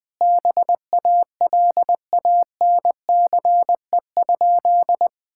Morse code, heard as a keyed tone, about 21 wpm, 700 Hz. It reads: BALANCE?
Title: Sentences from Top 1000 Words - Review Part 6 - 20wpm